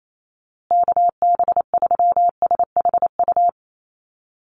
Morse code, heard as a keyed tone, about 28 wpm, 700 Hz. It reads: KB3SHU
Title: USA Call Signs Repeated 3x 28wpm